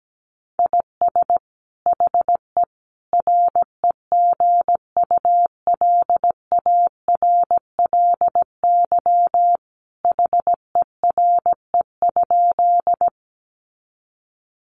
Morse code, heard as a keyed tone, about 17 wpm, 700 Hz. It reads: IS HE REGULARLY HERE?
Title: Sentences from Top 1400 Words 17wpm